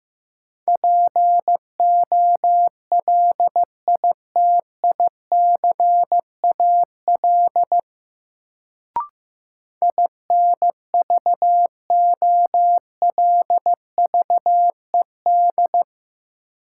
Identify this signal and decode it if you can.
Morse code, heard as a keyed tone, about 15 wpm, 700 Hz. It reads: POLITICAL INVOLVED
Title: Sentences from Top 800 Words - New Word Drill - 15wpm